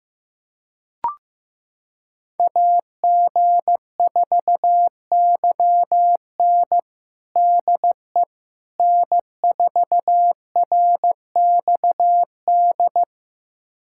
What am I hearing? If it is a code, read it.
Morse code, heard as a keyed tone, about 15 wpm, 700 Hz. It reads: AG4YN DE N4RXD
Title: USA Callsign DE Callsign - 5 Characters - 15wpm